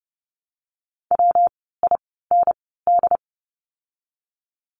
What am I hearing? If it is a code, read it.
Morse code, heard as a keyed tone, about 30 wpm, spaced as for 17 wpm, 700 Hz. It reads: WIND
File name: course / 30:10wpm / 115 Lesson 19f - Letter w - Words 30wpm